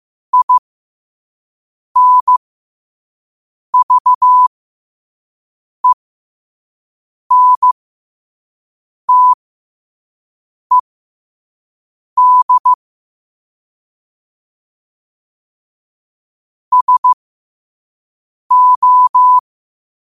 Morse code, heard as a keyed tone, about 15 wpm, spaced as for 5 wpm, 1000 Hz. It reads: INVENTED SO